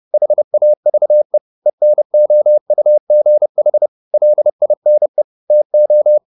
Morse code, heard as a keyed tone, about 30 wpm, 600 Hz. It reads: HAVE ENOUGH LINE TO
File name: the old man and the sea / the old man and the sea 1 30wpm